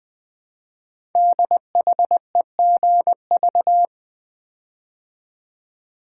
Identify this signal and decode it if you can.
Morse code, heard as a keyed tone, about 20 wpm, 700 Hz. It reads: DHEGV
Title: Code Groups - 5 Letters - 20wpm